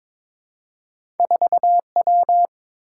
Morse code, heard as a keyed tone, about 22 wpm, 700 Hz. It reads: 4W
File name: practice / 22wpm / Two Letter-Number Combinations - 22wpm